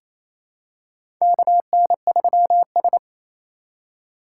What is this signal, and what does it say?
Morse code, heard as a keyed tone, about 28 wpm, 700 Hz. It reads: KN3S
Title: USA Call Signs - 3x Character Spacing - 28wpm